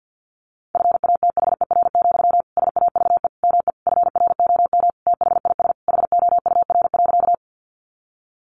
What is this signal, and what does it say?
Morse code, heard as a keyed tone, about 50 wpm, 700 Hz. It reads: FATHER, SAVE ME FROM THIS HOUR.
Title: Sentences from Top 1100 Words 50wpm